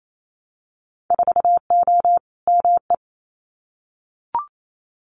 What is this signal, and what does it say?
Morse code, heard as a keyed tone, about 28 wpm, 700 Hz. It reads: 4O ME
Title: Winter Field Day Exchanges Encore 28wpm